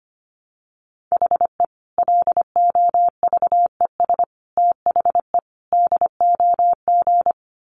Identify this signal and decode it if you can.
Morse code, heard as a keyed tone, about 25 wpm, 700 Hz. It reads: HE LOVES THE DOG
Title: Additional Easy Phrases 25wpm